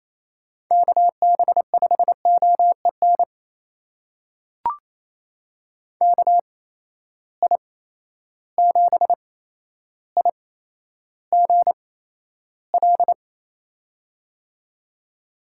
Morse code, heard as a keyed tone, about 28 wpm, spaced as for 12 wpm, 700 Hz. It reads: KB5OEN KI7IGL
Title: USA Call Signs - 6 Characters - 8x Character Spacing - 28wpm